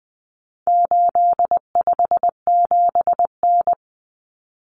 Morse code, heard as a keyed tone, about 20 wpm, 700 Hz. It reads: 857N
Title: Common Cut Numbers - 4-Digit 20wpm